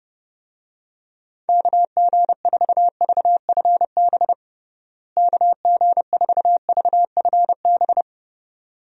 Morse code, heard as a keyed tone, about 30 wpm, 700 Hz. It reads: KG4VFB KG4VFB
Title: USA Call Signs - 6 Characters - Repeated 2x - 30wpm